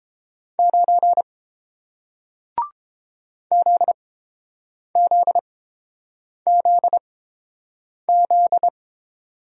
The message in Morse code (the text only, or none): 9 ZZZZ